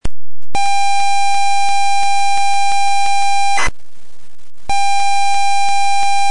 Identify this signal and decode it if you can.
ZX Spectrum tape loading.